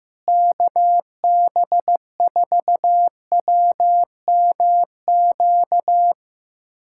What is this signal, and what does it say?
Morse code, heard as a keyed tone, about 15 wpm, 700 Hz. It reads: KB4WMQ
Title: USA Callsign DE Callsign - 6 Characters - 15wpm